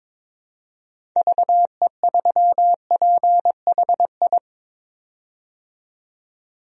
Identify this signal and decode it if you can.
Morse code, heard as a keyed tone, about 22 wpm, 700 Hz. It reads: VE3PHI
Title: DXCC-100 Call Signs - 6 Characters - 22wpm